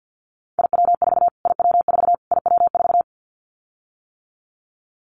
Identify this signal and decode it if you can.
Morse code, heard as a keyed tone, about 50 wpm, 700 Hz. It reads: IW4 IW4 IW4
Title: Callsign Prefixes for Top 25 DXCC - Repeated 3x - 50wpm